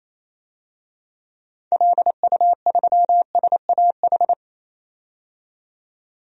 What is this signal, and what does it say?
Morse code, heard as a keyed tone, about 28 wpm, 700 Hz. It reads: LU3SAH
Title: DXCC-100 Call Signs - 6 Characters - 28wpm